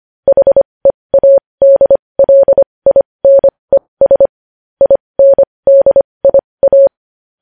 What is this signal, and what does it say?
Morse code, heard as a keyed tone, about 25 wpm, spaced as for 20 wpm, 550 Hz. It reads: HEADLINES INDIA
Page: News headlines in Morse code at 20 wpm